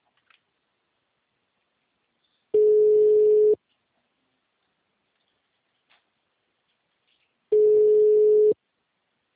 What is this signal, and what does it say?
Ringback tone (Europe, 425 Hz).
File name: out- -2032